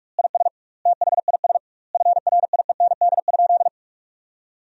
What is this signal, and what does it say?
Morse code, heard as a keyed tone, about 46 wpm, 700 Hz. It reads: IS THIS FRIEND?
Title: 2x Word-Spacing - Sentences from Top 300 Words - 45wpm